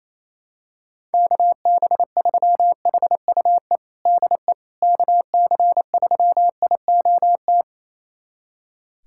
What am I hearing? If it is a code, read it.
Morse code, heard as a keyed tone, about 28 wpm, 700 Hz. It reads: KB3HUE DE KC3IOT
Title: USA Callsign DE Callsign - 6 Characters - 28wpm